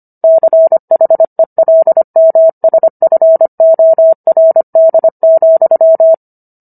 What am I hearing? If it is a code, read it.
Morse code, heard as a keyed tone, about 25 wpm, 650 Hz. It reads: CHELMSFORD,